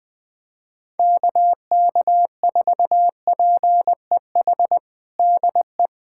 Morse code, heard as a keyed tone, about 20 wpm, 700 Hz. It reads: KK4PEH DE
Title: USA Callsign DE Callsign - 6 Characters - 20wpm